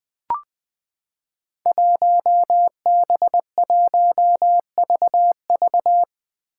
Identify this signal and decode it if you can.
Morse code, heard as a keyed tone, about 21 wpm, 700 Hz. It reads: 1B1VV
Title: Five Challenging Letter-Number Combinations 20wpm